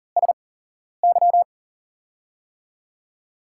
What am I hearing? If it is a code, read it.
Morse code, heard as a keyed tone, about 38 wpm, spaced as for 11 wpm, 700 Hz. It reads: SY
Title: USA Call Signs - 6 Characters - 8x Character Spacing - 40wpm